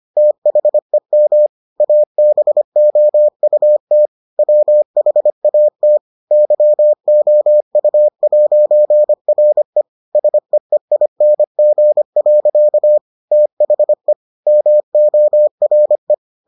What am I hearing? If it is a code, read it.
Morse code, heard as a keyed tone, about 25 wpm, 600 Hz. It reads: THEM ABOUT WHAT YOU'RE SEEING. THE MORE